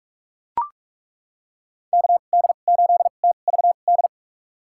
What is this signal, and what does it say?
Morse code, heard as a keyed tone, about 43 wpm, spaced as for 32 wpm, 700 Hz. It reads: KD8TVD
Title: USA Call Signs - 6 Characters - 2x Character Spacing - 45wpm